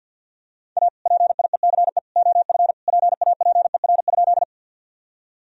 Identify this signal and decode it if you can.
Morse code, heard as a keyed tone, about 50 wpm, 700 Hz. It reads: A PIECE OF PAPER?
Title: Sentences from Top 1000 Words - Review Part 2 - 50wpm